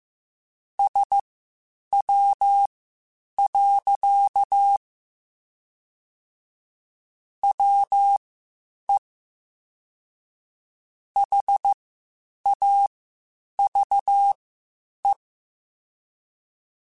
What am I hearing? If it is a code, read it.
Morse code, heard as a keyed tone, about 15 wpm, spaced as for 8 wpm, 800 Hz. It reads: SW. WE HAVE
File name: Spaced Characters / Spaced Characters-mp3s / Weather 5 5 sp @13